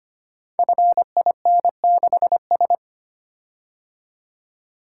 Morse code, heard as a keyed tone, about 25 wpm, 690 Hz. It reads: FIN6S
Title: Code Groups - 5 Letter:Numbers - 25wpm